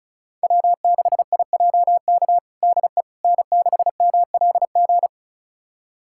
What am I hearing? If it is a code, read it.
Morse code, heard as a keyed tone, about 34 wpm, 700 Hz. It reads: W6IJK DE N6MLG
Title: USA Callsign DE Callsign - 5 Characters - 35wpm